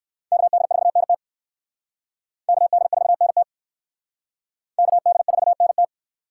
Morse code, heard as a keyed tone, about 58 wpm, 700 Hz. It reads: KD4NT KD4NT KD4NT